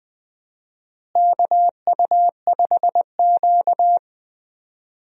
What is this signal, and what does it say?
Morse code, heard as a keyed tone, about 20 wpm, 700 Hz. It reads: KU5Q
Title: USA Call Signs - 4 Characters - Repeated 3x - 20wpm